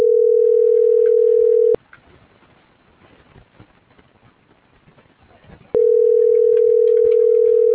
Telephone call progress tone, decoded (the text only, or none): ringback tone